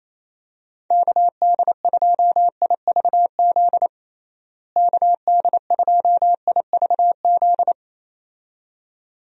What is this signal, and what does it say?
Morse code, heard as a keyed tone, about 27 wpm, 700 Hz. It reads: KD2IVZ KD2IVZ
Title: USA Call Signs - 6 Characters - Repeated 2x - 28wpm